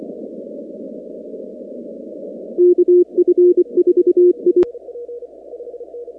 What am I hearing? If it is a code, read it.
Morse code, heard as a keyed tone, about 24 wpm, 360 Hz. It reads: KF4I
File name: contests / 2019-cw-wpx-kt5j-k5pi- -2330 0-19